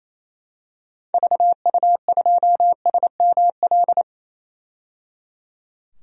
Morse code, heard as a keyed tone, about 28 wpm, 700 Hz. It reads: VU2SML
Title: DXCC-100 Call Signs - 6 Characters - 28wpm